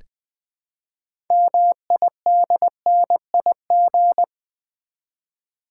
Morse code, heard as a keyed tone, about 20 wpm, 700 Hz. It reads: MIDNIG
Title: Build A Word - 2 Letters At A Time - 8-Letter Words - 20wpm